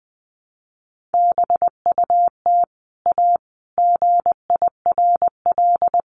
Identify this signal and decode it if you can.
Morse code, heard as a keyed tone, about 20 wpm, 700 Hz. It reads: BUT A GIRL
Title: Sentences from Top 1200 Words 20wpm